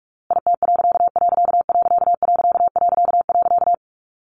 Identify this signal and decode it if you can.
Morse code, heard as a keyed tone, about 44 wpm, 700 Hz. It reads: IT......
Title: A Sound Of Thunder 45wpm